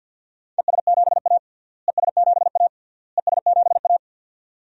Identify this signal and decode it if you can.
Morse code, heard as a keyed tone, about 51 wpm, 700 Hz. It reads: ES7A ES7A ES7A
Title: DXCC-100 Call Signs - 4 Characters - Repeated 3x - 50wpm